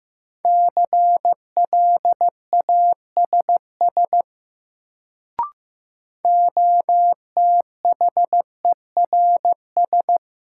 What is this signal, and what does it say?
Morse code, heard as a keyed tone, about 15 wpm, 700 Hz. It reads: CLASS OTHERS
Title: Sentences from Top 1000 Words - Review Part 2 - New Word Drill - 15wpm